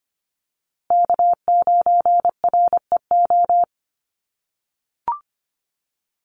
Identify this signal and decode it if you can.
Morse code, heard as a keyed tone, about 25 wpm, 700 Hz. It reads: K9REO
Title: USA Call Signs Repeated 3x 25wpm